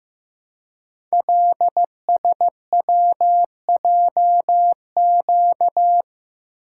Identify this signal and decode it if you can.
Morse code, heard as a keyed tone, about 15 wpm, 700 Hz. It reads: LSWJQ
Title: Code Groups - 5 Letter:Numbers - 15wpm